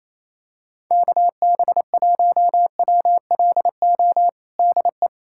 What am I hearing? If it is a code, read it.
Morse code, heard as a keyed tone, about 28 wpm, 700 Hz. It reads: KB1WLO DE
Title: USA Callsign DE Callsign 28wpm